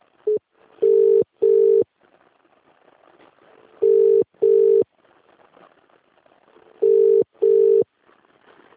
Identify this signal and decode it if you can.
Ringback tone (UK), 400 + 450 Hz.